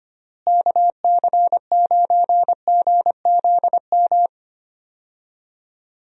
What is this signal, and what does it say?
Morse code, heard as a keyed tone, about 25 wpm, 700 Hz. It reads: KC9GZM
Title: USA Call Signs - 6 Characters - Repeated 3x - 25wpm